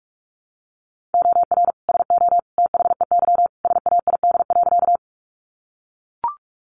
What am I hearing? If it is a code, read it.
Morse code, heard as a keyed tone, about 44 wpm, 700 Hz. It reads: OR SO THEY SAID.
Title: Sentences from Top 100 Words 45wpm